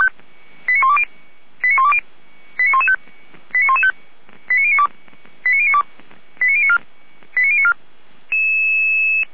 Five-tone selective call (ZVEI), heard as quick runs of five tones, tones of about 80 ms. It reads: 89120 89120 89115 89092 89092 89094 89094